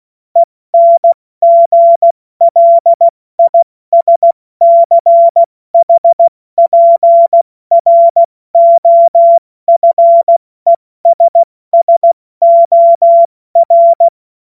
Morse code, heard as a keyed tone, about 16 wpm, 680 Hz. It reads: ENGLISCHPROFESSOR
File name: Morsekurs / Witze / Witz 10.Oktober 2023 70 F80